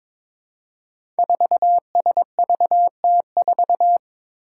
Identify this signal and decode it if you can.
Morse code, heard as a keyed tone, about 22 wpm, 700 Hz. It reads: 4SVT4
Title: Five Challenging Letter-Number Combinations 22wpm